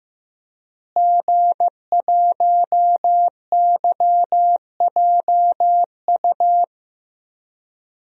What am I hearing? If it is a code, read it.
Morse code, heard as a keyed tone, about 16 wpm, 700 Hz. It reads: G1YJU
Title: DXCC-100 Call Signs - 5 Characters - Repeated 3x - 15wpm